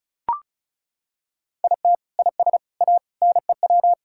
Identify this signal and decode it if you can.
Morse code, heard as a keyed tone, about 36 wpm, 700 Hz. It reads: IT IS A NEW
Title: Sentences from Top 1000 Words - Review Part 6 - 35wpm